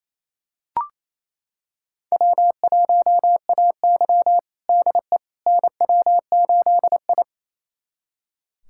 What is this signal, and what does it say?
Morse code, heard as a keyed tone, about 28 wpm, 700 Hz. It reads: W1AY DE NW8I
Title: USA Callsign DE Callsign - 4 Characters - 28wpm